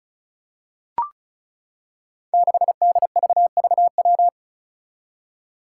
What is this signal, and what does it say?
Morse code, heard as a keyed tone, about 36 wpm, 700 Hz. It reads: 6DVVW
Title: Five Challenging Letter-Number Combinations 35wpm